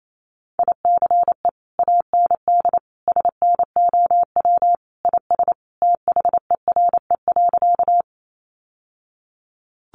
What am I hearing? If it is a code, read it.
Morse code, heard as a keyed tone, about 28 wpm, 700 Hz. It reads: ICE AND SNOW IS THERE.
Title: Sentences from Top 1000 Words 28wpm